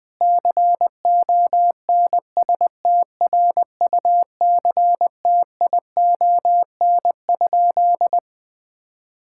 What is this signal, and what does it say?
Morse code, heard as a keyed tone, about 20 wpm, 700 Hz. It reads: CONSTRUCTION?
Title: Sentences from Top 1600 Words 20wpm